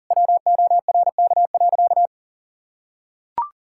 Morse code, heard as a keyed tone, about 37 wpm, 700 Hz. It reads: WORK.
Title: Sentences from Top 1000 Words - Review Part 6 - 40wpm